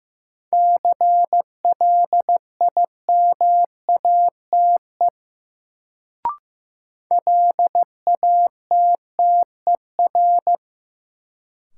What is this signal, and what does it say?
Morse code, heard as a keyed tone, about 15 wpm, 700 Hz. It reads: CLIMATE LATTER
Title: Sentences from Top 1600 Words - New Word Drill Alt - 15wpm